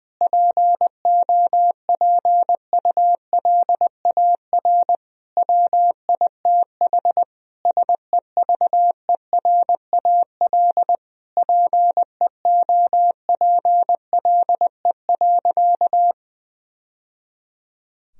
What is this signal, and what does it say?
Morse code, heard as a keyed tone, about 20 wpm, 700 Hz. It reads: POPULAR WITH SEVERAL PEOPLE.
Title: Sentences from Top 1000 Words - Review Part 3 - 20wpm